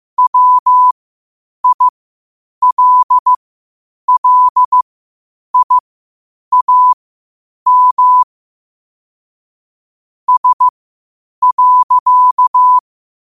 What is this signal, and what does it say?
Morse code, heard as a keyed tone, about 15 wpm, spaced as for 8 wpm, 1000 Hz. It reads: WILLIAM S.
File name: qotdfiles / 8wpm